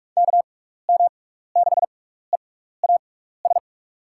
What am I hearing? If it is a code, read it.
Morse code, heard as a keyed tone, about 46 wpm, spaced as for 16 wpm, 700 Hz. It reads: KM6EAS
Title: USA Call Signs - 6 Characters - 6x Character Spacing - 45wpm